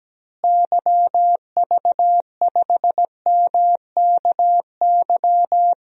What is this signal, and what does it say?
Morse code, heard as a keyed tone, about 17 wpm, 700 Hz. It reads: YV5MKY
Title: DXCC-100 Call Signs - 6 Characters - Repeated 3x - 17wpm